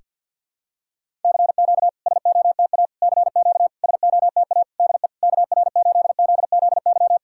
Morse code, heard as a keyed tone, about 50 wpm, 700 Hz. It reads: CQ SOTA CQ SOTA DE KR8ZZY
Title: SOTA QSO Practice 50wpm